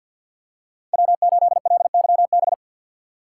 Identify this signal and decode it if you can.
Morse code, heard as a keyed tone, about 49 wpm, 700 Hz. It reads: W8LYB
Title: USA Call Signs - 5 Characters - 4x Character Spacing - 50wpm